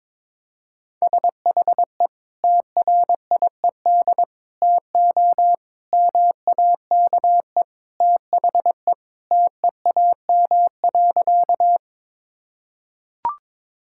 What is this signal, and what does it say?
Morse code, heard as a keyed tone, about 22 wpm, 700 Hz. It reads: SHE TRIED TO MAKE THE TEAM.
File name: practice / 22wpm / Sentences with 3-Word Phrases - 22wpm